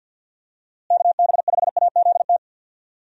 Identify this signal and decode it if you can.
Morse code, heard as a keyed tone, about 50 wpm, 700 Hz. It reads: KB5AZT
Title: USA Call Signs - 6 Characters - Repeated 3x - 50wpm